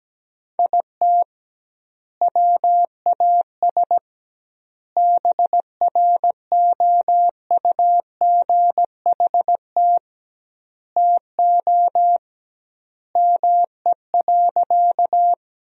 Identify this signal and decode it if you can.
Morse code, heard as a keyed tone, about 17 wpm, 700 Hz. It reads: IT WAS BROUGHT TO ME.
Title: 2x Word-Spacing - Sentences from Top 500 Words - 17wpm